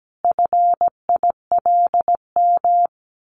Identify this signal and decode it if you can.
Morse code, heard as a keyed tone, about 17 wpm, 700 Hz. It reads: FILM
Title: A Sound Of Thunder 17wpm